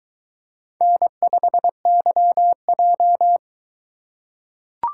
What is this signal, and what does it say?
Morse code, heard as a keyed tone, about 22 wpm, 700 Hz. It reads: N5YJ